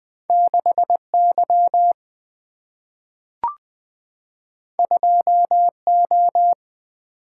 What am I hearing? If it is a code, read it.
Morse code, heard as a keyed tone, about 20 wpm, 700 Hz. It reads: 6 Y 2 O
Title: Single-Letter Single-Number 20wpm